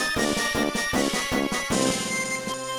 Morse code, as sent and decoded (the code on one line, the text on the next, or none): none